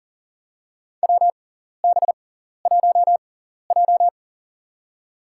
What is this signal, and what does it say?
Morse code, heard as a keyed tone, about 40 wpm, spaced as for 14 wpm, 700 Hz. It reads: WB1J
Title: USA Call Signs - 4 Characters - 6x Character Spacing - 40wpm